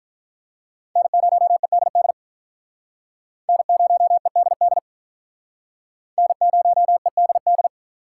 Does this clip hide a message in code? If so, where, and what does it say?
Yes: Morse code from 1.0 to 7.7 s, heard as a keyed tone, about 44 wpm, 700 Hz. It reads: N0EDD N0EDD N0EDD